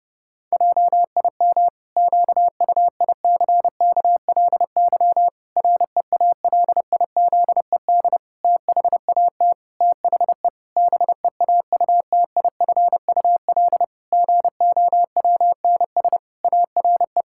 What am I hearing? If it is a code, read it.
Morse code, heard as a keyed tone, about 30 wpm, 700 Hz. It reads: JIM QUICKLY REALIZED THAT THE BEAUTIFUL GOWNS ARE
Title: Pangram Challenge v2 - 30wpm